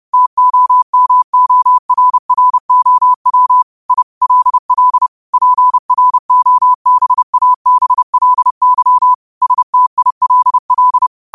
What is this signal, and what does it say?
Morse code, heard as a keyed tone, about 30 wpm, 1000 Hz. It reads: TOMORROW I LL PROBABLY STILL
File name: qotdfiles / 30wpm